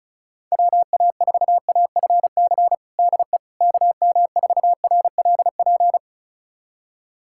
Morse code, heard as a keyed tone, about 34 wpm, 700 Hz. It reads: WA4AFC DE KM4RLP
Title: USA Callsign DE Callsign - 6 Characters - 35wpm